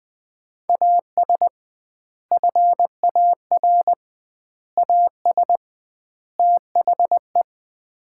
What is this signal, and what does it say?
Morse code, heard as a keyed tone, about 20 wpm, 700 Hz. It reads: AS FAR AS THE
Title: 2x Word-Spacing - Sets of 4 Words - 20wpm